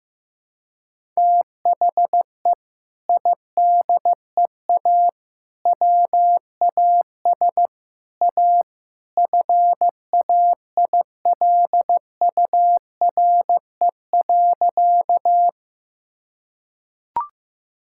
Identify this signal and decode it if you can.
Morse code, heard as a keyed tone, about 15 wpm, 700 Hz. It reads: THE IDEA WAS A FAILURE.
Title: Sentences from Top 1000 Words - Review Part 6 - 15wpm